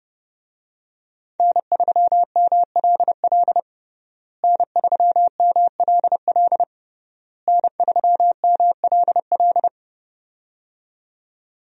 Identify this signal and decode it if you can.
Morse code, heard as a keyed tone, about 30 wpm, 700 Hz. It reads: N3MLL N3MLL N3MLL
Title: USA Call Signs - 5 Characters - Repeated 3x - 30wpm